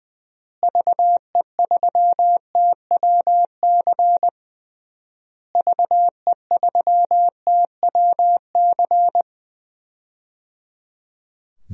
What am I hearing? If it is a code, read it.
Morse code, heard as a keyed tone, about 20 wpm, 700 Hz. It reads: VE3TWC VE3TWC
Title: DXCC-100 Call Signs - 6 Characters - Repeated 3x - 20wpm